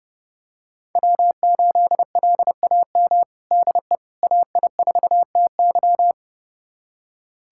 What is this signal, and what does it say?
Morse code, heard as a keyed tone, about 30 wpm, 700 Hz. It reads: W8LAM DE AI4TY
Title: USA Callsign DE Callsign - 5 Characters - 30wpm